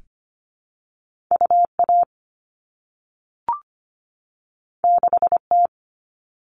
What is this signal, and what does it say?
Morse code, heard as a keyed tone, about 25 wpm, 700 Hz. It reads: UA 6T
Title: All Cut Numbers - 2-Digit 25wpm